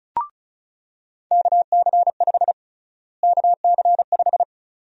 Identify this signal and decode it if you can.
Morse code, heard as a keyed tone, about 36 wpm, 700 Hz. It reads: KC5 KC5
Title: USA Call Sign Prefixes - Repeated 2x - 35wpm